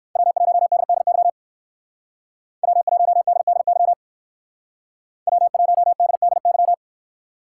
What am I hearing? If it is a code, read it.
Morse code, heard as a keyed tone, about 59 wpm, 700 Hz. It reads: W1DDY W1DDY W1DDY